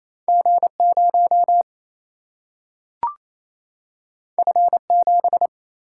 Morse code, heard as a keyed tone, about 28 wpm, 700 Hz. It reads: G 0 F 7
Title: Single-Letter Single-Number 28wpm